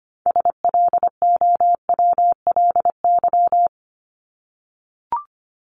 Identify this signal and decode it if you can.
Morse code, heard as a keyed tone, about 25 wpm, 700 Hz. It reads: SLOWLY
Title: Top 1001-1250 Words 25wpm